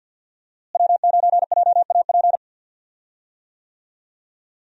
Morse code, heard as a keyed tone, about 46 wpm, 700 Hz. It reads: W9JAP
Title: USA Call Signs - 5 Characters - Repeated 3x - 50wpm